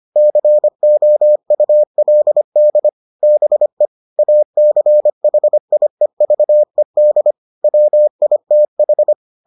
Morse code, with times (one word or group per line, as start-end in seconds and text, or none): 0.2-2.9 COULD
3.2-3.9 BE
4.2-7.3 ACHIEVED
7.6-9.1 WITH